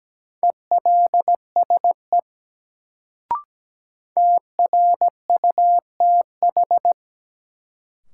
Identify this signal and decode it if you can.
Morse code, heard as a keyed tone, about 17 wpm, 700 Hz. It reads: ELSE TRUTH
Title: Sentences from Top 800 Words - New Word Drill - 17wpm